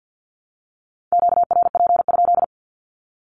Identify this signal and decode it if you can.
Morse code, heard as a keyed tone, about 50 wpm, 700 Hz. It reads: QRP?
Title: Q-Codes 50wpm